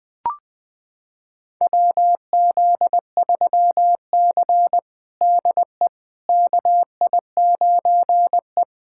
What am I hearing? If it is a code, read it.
Morse code, heard as a keyed tone, about 20 wpm, 700 Hz. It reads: WZ3C DE KI9E
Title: USA Callsign DE Callsign - 4 Characters - 20wpm